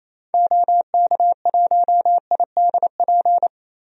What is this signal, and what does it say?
Morse code, heard as a keyed tone, about 28 wpm, 700 Hz. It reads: OK1IDP
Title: DXCC-100 Call Signs - 6 Characters - 28wpm